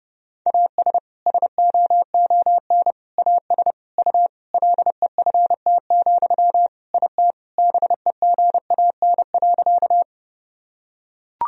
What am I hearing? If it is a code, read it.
Morse code, heard as a keyed tone, about 30 wpm, 700 Hz. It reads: AS SOON AS U LEFT, IT BEGAN.
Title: Sentences from Top 400 Words and Abbreviations 30wpm